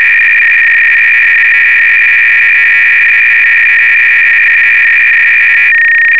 300-baud modem data.
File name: Repository / ACR-Audio-Cassette-Programs / MS-8K-BASIC-V4